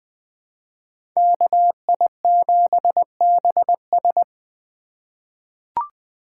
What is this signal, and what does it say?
Morse code, heard as a keyed tone, about 20 wpm, 700 Hz. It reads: KI7BS
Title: USA Call Signs - 5 Characters - 2x Character Spacing - 20wpm